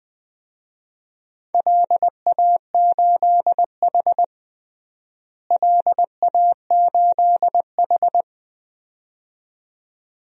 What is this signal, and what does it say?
Morse code, heard as a keyed tone, about 20 wpm, 700 Hz. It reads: LA8H LA8H
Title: DXCC-100 Call Signs - 4 Characters - Repeated 2x - 20wpm